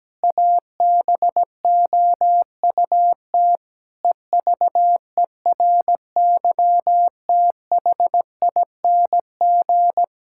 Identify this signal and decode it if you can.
Morse code, heard as a keyed tone, about 17 wpm, 700 Hz. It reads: ABOUT EVERYTHING
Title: Sentences from Top 1600 Words 17wpm